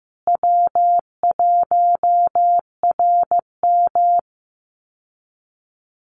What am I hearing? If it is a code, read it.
Morse code, heard as a keyed tone, about 15 wpm, 700 Hz. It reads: W1RM
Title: Easy Call Signs Repeated 3x - 15wpm